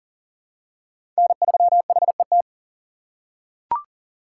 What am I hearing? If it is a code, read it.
Morse code, heard as a keyed tone, about 40 wpm, 700 Hz. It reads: N3HET
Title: USA Call Signs - 5 Characters - 2x Character Spacing - 40wpm